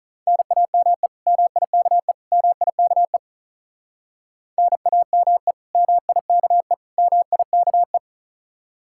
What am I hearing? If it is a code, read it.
Morse code, heard as a keyed tone, about 45 wpm, 700 Hz. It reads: NAME MIKE MIKE NAME MIKE MIKE